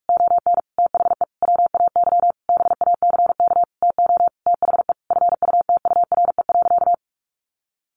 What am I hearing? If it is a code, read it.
Morse code, heard as a keyed tone, about 44 wpm, 700 Hz. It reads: ON THE WAY BACK TO THE FUTURE.
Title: A Sound Of Thunder 45wpm